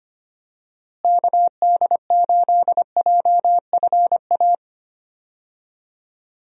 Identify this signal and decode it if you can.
Morse code, heard as a keyed tone, about 25 wpm, 700 Hz. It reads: KD8JFA
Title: USA Call Signs - 6 Characters - Repeated 3x - 25wpm